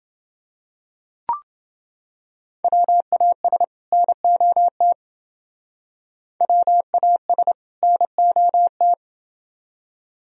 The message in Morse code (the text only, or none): WAS NOT WAS NOT